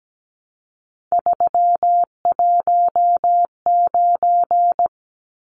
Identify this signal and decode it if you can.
Morse code, heard as a keyed tone, about 17 wpm, 700 Hz. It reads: 319
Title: Numbers - 3-Digit 17wpm